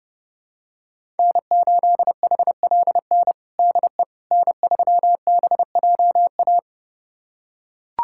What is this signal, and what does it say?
Morse code, heard as a keyed tone, about 30 wpm, 700 Hz. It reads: N8HLN DE N3BJA
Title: USA Callsign DE Callsign - 5 Characters - 30wpm